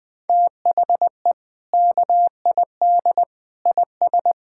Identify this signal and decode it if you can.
Morse code, heard as a keyed tone, about 20 wpm, 700 Hz. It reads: THE KID IS
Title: Sentences from Top 1600 Words 20wpm